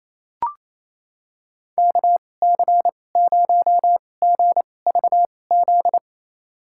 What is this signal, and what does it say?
Morse code, heard as a keyed tone, about 28 wpm, spaced as for 20 wpm, 700 Hz. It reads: KC0GVZ
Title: USA Call Signs - 2x Character Spacing - 28wpm